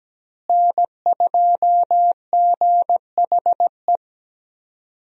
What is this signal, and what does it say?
Morse code, heard as a keyed tone, about 17 wpm, 700 Hz. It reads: N2GHE
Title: USA Call Signs - 5 Characters - Repeated 3x - 17wpm